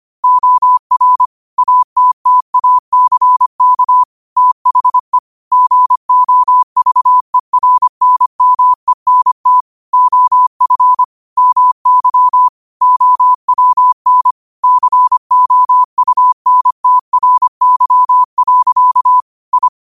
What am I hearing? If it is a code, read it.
Morse code, heard as a keyed tone, about 25 wpm, 1000 Hz. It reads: OR ATTACK THE GOVERNMENT OF MY OWN COUNTRY. I
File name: qotdfiles / 25wpm